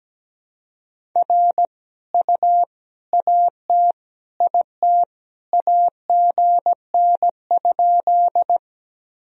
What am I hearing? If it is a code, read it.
Morse code, heard as a keyed tone, about 17 wpm, 700 Hz. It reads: R U AT IT AGN?
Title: Sentences from Top 200 Words and Abbreviations 17wpm